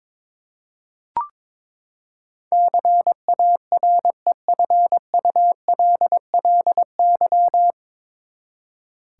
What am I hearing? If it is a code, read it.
Morse code, heard as a keyed tone, about 22 wpm, 700 Hz. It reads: CAREFULLY
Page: Morse Code Ninja practice 22wpm